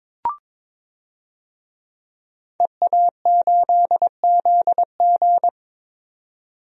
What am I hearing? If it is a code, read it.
Morse code, heard as a keyed tone, about 22 wpm, 700 Hz. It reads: EA8ZG
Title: DXCC-100 Call Signs - 5 Characters - 22wpm